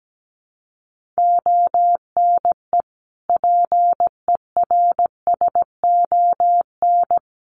Morse code, heard as a keyed tone, about 17 wpm, 700 Hz. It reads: ONE PERSON
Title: Sentences from Top 1500 Words 17wpm